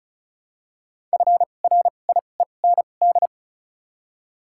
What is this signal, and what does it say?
Morse code, heard as a keyed tone, about 36 wpm, spaced as for 25 wpm, 700 Hz. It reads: FRIEND
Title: 2x Character Spacing - Top 500 Words - 35wpm